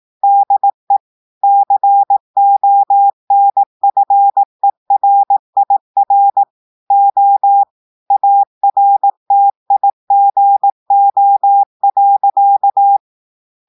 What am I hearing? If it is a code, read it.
Morse code, heard as a keyed tone, about 18 wpm, 800 Hz. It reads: DE CONFERIR O ARTIGO.
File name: audios / portugais / portugais texte en clair 2 vitesse 18 mots minutes